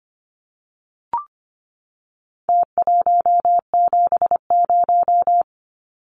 Morse code, heard as a keyed tone, about 25 wpm, 700 Hz. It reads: T170
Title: Numbers - Contest Style - 4-Digit 25wpm